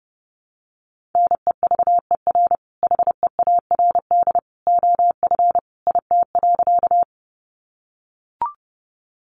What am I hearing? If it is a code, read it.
Morse code, heard as a keyed tone, about 30 wpm, 700 Hz. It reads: NEVER HEARD OF IT.
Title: Sentences from Top 400 Words 30wpm